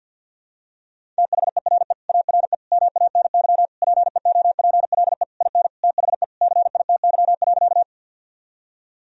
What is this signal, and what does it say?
Morse code, heard as a keyed tone, about 52 wpm, 700 Hz. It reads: THERE ARE MANY PEOPLE IN THE CITY.
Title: Sentences from Top 1000 Words - Review Part 1 - 50wpm